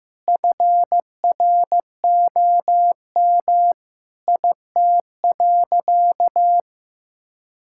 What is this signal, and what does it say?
Morse code, heard as a keyed tone, about 16 wpm, 700 Hz. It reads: FROM IT.
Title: Sentences from Top 1000 Words - Review Part 5 - 15wpm